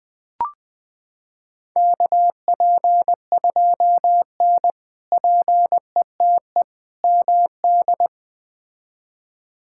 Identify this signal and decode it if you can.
Morse code, heard as a keyed tone, about 20 wpm, 700 Hz. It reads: KP2N PETE MD
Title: K1USN SST QSO Practice - 20wpm